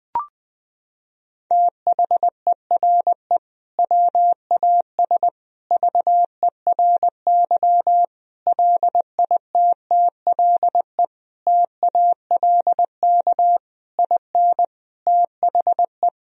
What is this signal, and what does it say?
Morse code, heard as a keyed tone, about 20 wpm, 700 Hz. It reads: THERE WAS VERY LITTLE TALK IN THE
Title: Sentences from Top 1000 Words - Review Part 1 - 20wpm